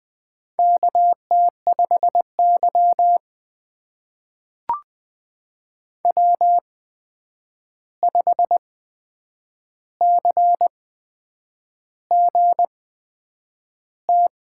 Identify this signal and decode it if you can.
Morse code, heard as a keyed tone, about 20 wpm, spaced as for 8 wpm, 700 Hz. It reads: KT5Y W5CGT
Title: USA Call Signs - 8x Character Spacing - 20wpm